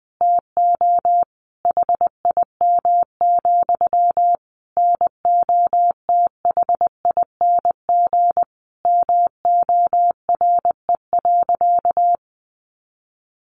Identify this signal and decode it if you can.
Morse code, heard as a keyed tone, about 20 wpm, 700 Hz. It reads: TO HIM, NOTHING MORE.
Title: Sentences from Top 1300 Words 20wpm